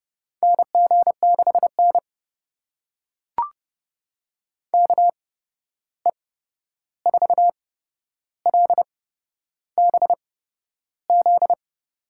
Morse code, heard as a keyed tone, about 30 wpm, spaced as for 11 wpm, 700 Hz. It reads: NG6N KE4LBZ
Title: USA Call Signs - 8x Character Spacing - 30wpm